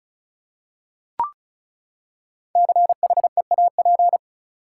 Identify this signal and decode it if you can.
Morse code, heard as a keyed tone, about 34 wpm, 700 Hz. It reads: CHEAP
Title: Five Letter Words - 35wpm